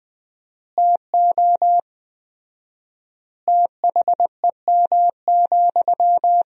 Morse code, heard as a keyed tone, about 20 wpm, 700 Hz. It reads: TO THEM,
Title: 4x Word-Spacing - Sentences from Top 400 Words - 20wpm